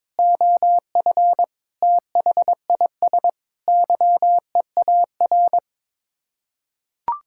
Morse code, heard as a keyed tone, about 22 wpm, 700 Hz. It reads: OF THIS YEAR
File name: practice / 22wpm / Sets of 8 Words - 22wpm